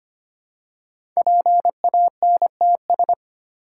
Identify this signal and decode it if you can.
Morse code, heard as a keyed tone, about 25 wpm, 700 Hz. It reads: PANTS
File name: 5 Letter Words - First 3 Letters Given - 25wpm